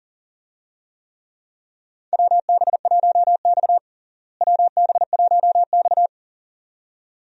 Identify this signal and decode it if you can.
Morse code, heard as a keyed tone, about 40 wpm, 700 Hz. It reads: WB1X WB1X
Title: USA Call Signs - 4 Characters - Repeated 2x - 40wpm